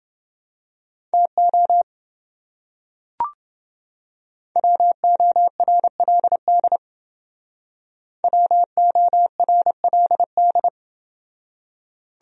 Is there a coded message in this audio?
Yes: Morse code from 1.1 to 10.7 s, heard as a keyed tone, about 27 wpm, 700 Hz. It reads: TO WORLD WORLD